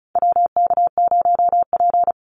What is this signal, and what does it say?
Morse code, heard as a keyed tone, about 34 wpm, 700 Hz. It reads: WK0P
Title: Call Signs Repeated 2x - 35wpm